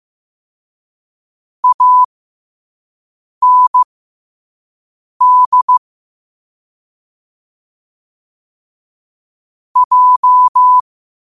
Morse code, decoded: AND J

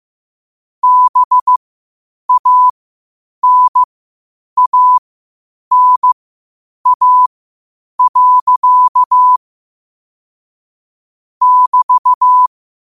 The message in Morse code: BANANA. <BT>

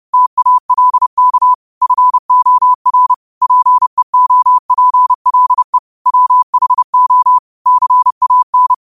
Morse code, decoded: TALK FOR PEOPLE WHO CAN

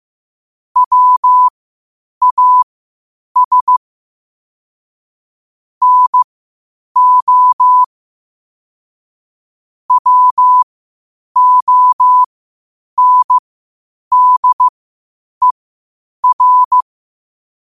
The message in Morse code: WAS NO WONDER

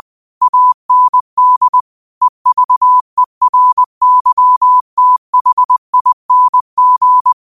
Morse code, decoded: AND EVERYTHING

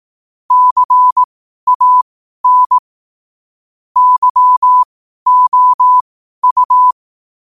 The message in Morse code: CAN YOU

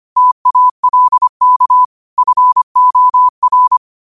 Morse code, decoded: TALK FOR

